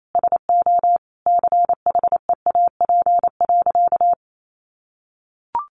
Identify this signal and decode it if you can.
Morse code, heard as a keyed tone, about 28 wpm, 700 Hz. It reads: SO CHEAP.
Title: Sentences from Top 1200 Words 28wpm